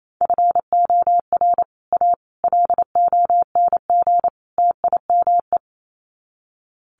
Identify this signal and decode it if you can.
Morse code, heard as a keyed tone, about 28 wpm, 700 Hz. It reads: FOR A LONG TIME
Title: Sets of 4 Words 28wpm